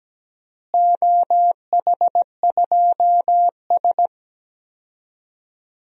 Morse code, heard as a keyed tone, about 17 wpm, 700 Hz. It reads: OH2S
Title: DXCC-100 Call Signs - 4 Characters - Repeated 2x - 17wpm